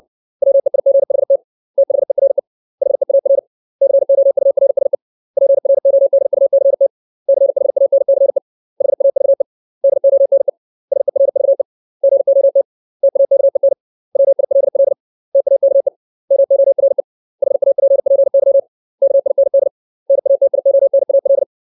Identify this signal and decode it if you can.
Morse code, heard as a keyed tone, about 60 wpm, 550 Hz. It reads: WEIGHT THERE HEAR COURSE PRODUCT CHANCE HAVE DONE SERVE GOT TAKEN WILL TAKE MORE HAPPY KIND NATIONAL
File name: 632 words / 632 Words 60WPM